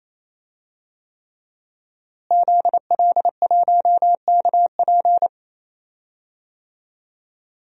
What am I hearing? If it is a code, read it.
Morse code, heard as a keyed tone, about 28 wpm, 700 Hz. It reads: ZL1KP
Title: DXCC-100 Call Signs - 5 Characters - Repeated 3x - 28wpm